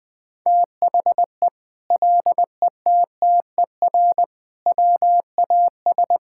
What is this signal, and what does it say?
Morse code, heard as a keyed tone, about 20 wpm, 700 Hz. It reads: THE LETTER WAS
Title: Sentences from Top 1000 Words - Review Part 5 - 20wpm